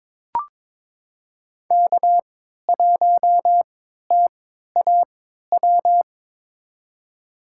Morse code, heard as a keyed tone, about 22 wpm, spaced as for 13 wpm, 700 Hz. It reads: K1TAW
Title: USA Call Signs - 3x Character Spacing - 22wpm